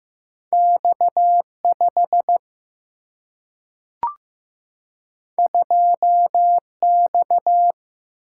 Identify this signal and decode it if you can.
Morse code, heard as a keyed tone, about 15 wpm, 700 Hz. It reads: X 5 2 X
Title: Single-Letter Single-Number 15wpm